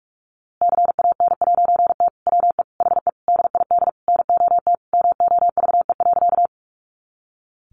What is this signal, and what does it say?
Morse code, heard as a keyed tone, about 47 wpm, 700 Hz. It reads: CAN'T WE HE DID NOT MOVE.
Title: A Sound Of Thunder 45wpm